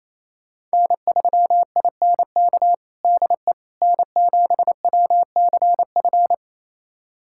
Morse code, heard as a keyed tone, about 28 wpm, 700 Hz. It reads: N3INK DE N7WCF
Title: USA Callsign DE Callsign - 5 Characters - 28wpm